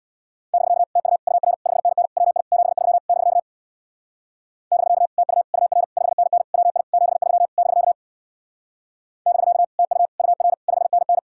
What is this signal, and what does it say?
Morse code, heard as a keyed tone, about 62 wpm, 700 Hz. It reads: <BK> TU UR 5NN WI 73 <BK> <BK> TU UR 5NN WI 73 <BK> <BK> TU UR 5NN
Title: Speed-Racing Common QSO Phrases - 50wpm